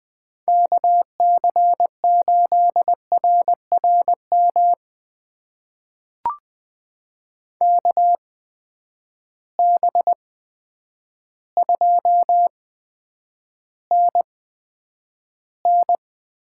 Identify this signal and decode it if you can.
Morse code, heard as a keyed tone, about 20 wpm, spaced as for 9 wpm, 700 Hz. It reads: KC8RRM KB2NN
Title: USA Call Signs - 6 Characters - 8x Character Spacing - 20wpm